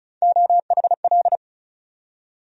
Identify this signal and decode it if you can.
Morse code, heard as a keyed tone, about 36 wpm, 700 Hz. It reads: OHL
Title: USA Call Sign Postfixes - 3 Letters 35wpm